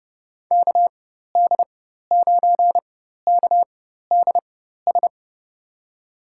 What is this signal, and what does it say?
Morse code, heard as a keyed tone, about 30 wpm, spaced as for 14 wpm, 700 Hz. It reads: KD9KDS
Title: USA Call Signs - 4x Character Spacing - 30wpm